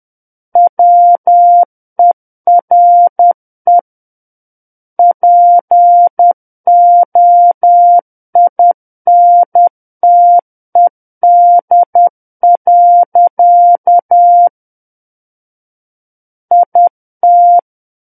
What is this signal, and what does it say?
Morse code, heard as a keyed tone, about 10 wpm, 700 Hz. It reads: WERE POINTED. IT
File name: Book One / 01-Chapter 1 - 10 WPM